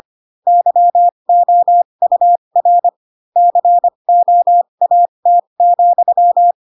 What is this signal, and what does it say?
Morse code, heard as a keyed tone, about 25 wpm, spaced as for 22 wpm, 700 Hz. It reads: YOUR COAT,